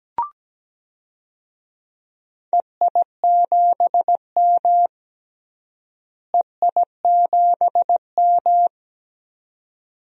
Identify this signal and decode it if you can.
Morse code, heard as a keyed tone, about 17 wpm, 700 Hz. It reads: EI7M EI7M
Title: DXCC-100 Call Signs - 4 Characters - Repeated 2x - 17wpm